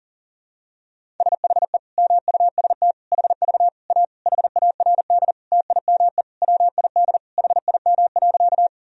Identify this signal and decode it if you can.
Morse code, heard as a keyed tone, about 37 wpm, 700 Hz. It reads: SHE MUST HV A HARD TIME WID HIM.
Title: Sentences from Top 300 Words and Abbreviations 40wpm